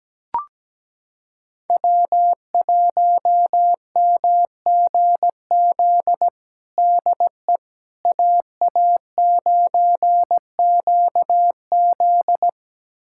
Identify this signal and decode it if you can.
Morse code, heard as a keyed tone, about 17 wpm, 700 Hz. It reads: W1MGZ DE AA9QZ
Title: USA Callsign DE Callsign - 5 Characters - 17wpm